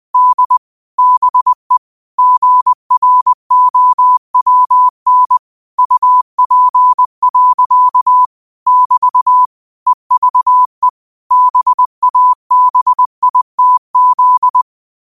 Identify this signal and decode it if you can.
Morse code, heard as a keyed tone, about 20 wpm, 1000 Hz. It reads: D BE GROWN UP. <BT> EVE BABITZ
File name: qotdfiles / 20wpm